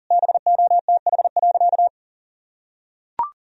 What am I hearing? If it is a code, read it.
Morse code, heard as a keyed tone, about 40 wpm, 700 Hz. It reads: BOTH.
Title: Sentences from Top 1000 Words - Review Part 1 - 40wpm